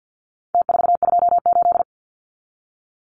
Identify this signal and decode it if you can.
Morse code, heard as a keyed tone, about 49 wpm, 700 Hz. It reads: T428
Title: Numbers - Contest Style - 4-Digit 50wpm